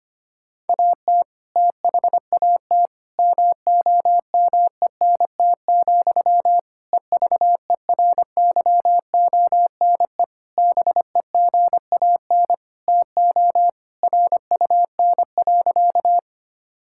Morse code, heard as a keyed tone, about 25 wpm, 700 Hz. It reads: AT THAT MOMENT, EVERYONE BEGAN TO RUN.
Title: Sentences from Top 1000 Words - Review Part 5 - 25wpm